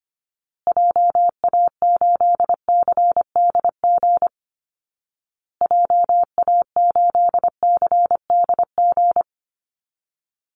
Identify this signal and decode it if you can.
Morse code, heard as a keyed tone, about 25 wpm, 700 Hz. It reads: JA8CDG JA8CDG
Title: Call Signs Repeated 2x - 25wpm